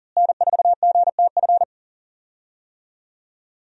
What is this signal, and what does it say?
Morse code, heard as a keyed tone, about 40 wpm, 700 Hz. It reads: N4GTF
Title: USA Call Signs - 5 Characters - Repeated 2x - 40wpm